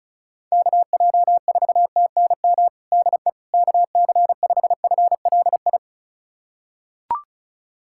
Morse code, heard as a keyed tone, about 36 wpm, 700 Hz. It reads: KJ4TNM DE KC5FLI
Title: USA Callsign DE Callsign - 6 Characters - 35wpm